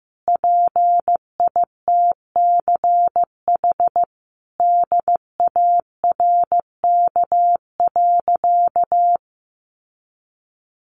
Morse code, heard as a keyed tone, about 15 wpm, 700 Hz. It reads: PITCH DARK.
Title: Sentences from Top 1300 Words 15wpm